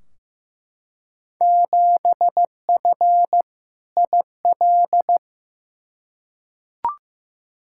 Morse code, heard as a keyed tone, about 15 wpm, 700 Hz. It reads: 7F IL
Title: Field Day Exchanges Encore - 15wpm